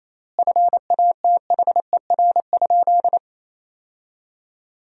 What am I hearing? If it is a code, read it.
Morse code, heard as a keyed tone, about 28 wpm, 700 Hz. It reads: FATHER?
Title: 3x Word-Spacing - Sentences from Top 400 Words - 28wpm